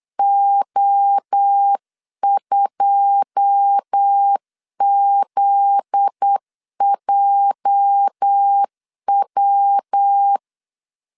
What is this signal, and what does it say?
Morse code, heard as a keyed tone, about 8 wpm, 800 Hz. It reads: O2ZJW
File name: CW / DET / 1 9 1500C S C 8wpm 5653